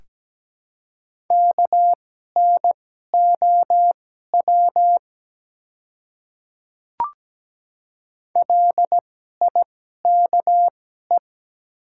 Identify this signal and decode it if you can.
Morse code, heard as a keyed tone, about 17 wpm, spaced as for 12 wpm, 700 Hz. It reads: KNOW LIKE
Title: 2x Character Spacing - Top 100 Words - 17wpm